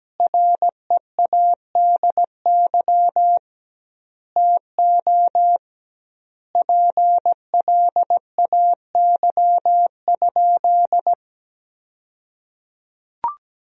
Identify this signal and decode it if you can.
Morse code, heard as a keyed tone, about 17 wpm, 700 Hz. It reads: READY TO PLAY?
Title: 2x Word-Spacing - Sentences from Top 400 Words - 17wpm